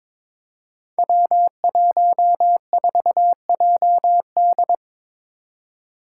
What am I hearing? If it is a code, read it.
Morse code, heard as a keyed tone, about 22 wpm, 700 Hz. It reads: W14JD
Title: Five Challenging Letter-Number Combinations 22wpm